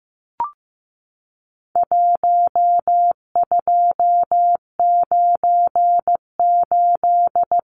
Morse code, heard as a keyed tone, about 15 wpm, 700 Hz. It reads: 1298
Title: Numbers - 4-Digit 15wpm